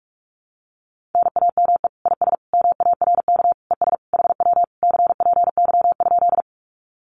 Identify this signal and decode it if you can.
Morse code, heard as a keyed tone, about 44 wpm, 700 Hz. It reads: NAME IS MARK ES HW CPY?
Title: Common QSO Phrases Encore 45wpm